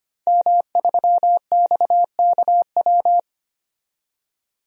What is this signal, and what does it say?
Morse code, heard as a keyed tone, about 25 wpm, 700 Hz. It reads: M3XKW
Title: DXCC-100 Call Signs - 5 Characters - 25wpm